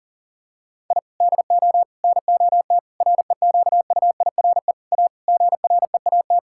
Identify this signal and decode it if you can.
Morse code, heard as a keyed tone, about 40 wpm, 700 Hz. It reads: I DO NOT REQUIRE A GREAT
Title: Sentences from Top 1000 Words - Review Part 6 - 40wpm